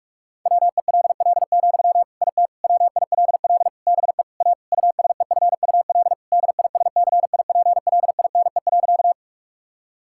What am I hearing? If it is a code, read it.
Morse code, heard as a keyed tone, about 46 wpm, 700 Hz. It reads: WELL, IT WILL BE A USEFUL DISCIPLINE.
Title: Sentences from Top 1000 Words - Review Part 6 - 45wpm